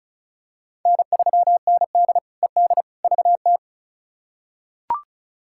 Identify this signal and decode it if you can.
Morse code, heard as a keyed tone, about 34 wpm, 700 Hz. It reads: N3ND ED VT
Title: K1USN SST QSO Practice - 35wpm